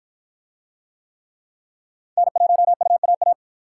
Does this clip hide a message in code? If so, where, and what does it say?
Yes: Morse code from 2.2 to 3.3 s, heard as a keyed tone, about 53 wpm, 700 Hz. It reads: N1UAA